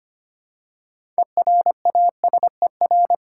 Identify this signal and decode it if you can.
Morse code, heard as a keyed tone, about 25 wpm, 700 Hz. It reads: ERASER
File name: practice / 25wpm / Easy Words with Hints - 25wpm